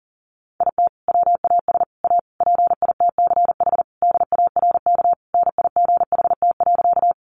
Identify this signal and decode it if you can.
Morse code, heard as a keyed tone, about 40 wpm, 700 Hz. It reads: IT WAS A PITCH DARK NIGHT.
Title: Sentences from Top 1300 Words 40wpm